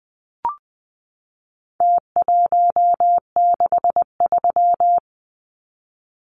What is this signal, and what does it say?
Morse code, heard as a keyed tone, about 20 wpm, 700 Hz. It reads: T163
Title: Numbers - Contest Style - 4-Digit 20wpm